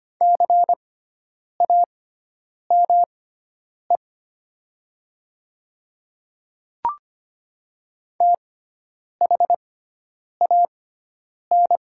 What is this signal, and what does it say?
Morse code, heard as a keyed tone, about 25 wpm, spaced as for 9 wpm, 700 Hz. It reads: CAME THAN
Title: 6x Character Spacing - Top 200 Words - 25wpm